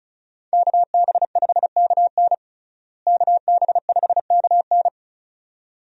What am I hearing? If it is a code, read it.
Morse code, heard as a keyed tone, about 34 wpm, 700 Hz. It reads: KB5KN KB5KN
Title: USA Call Signs - 5 Characters - Repeated 2x - 35wpm